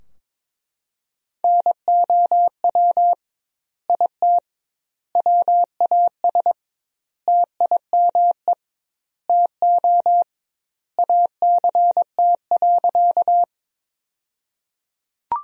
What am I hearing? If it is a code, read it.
Morse code, heard as a keyed tone, about 21 wpm, 700 Hz. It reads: NOW IT WAS TIME TO ACT.
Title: 2x Word-Spacing - Sentences from Top 200 Words - 22wpm